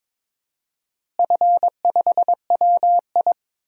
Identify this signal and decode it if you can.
Morse code, heard as a keyed tone, about 22 wpm, 700 Hz. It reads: F5WI
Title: DXCC-100 Call Signs - 4 Characters - Repeated 3x - 22wpm